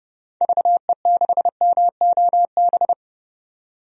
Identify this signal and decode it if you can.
Morse code, heard as a keyed tone, about 30 wpm, 700 Hz. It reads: VE6MOB
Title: DXCC-100 Call Signs - 6 Characters - 30wpm